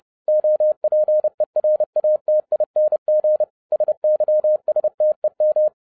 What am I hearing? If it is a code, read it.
Morse code, heard as a keyed tone, about 30 wpm, 610 Hz. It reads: OPERATING SYSTEM
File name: 30wpm